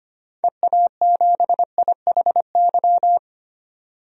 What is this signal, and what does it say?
Morse code, heard as a keyed tone, about 25 wpm, 700 Hz. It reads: EA7IHY
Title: DXCC-100 Call Signs - 6 Characters - 25wpm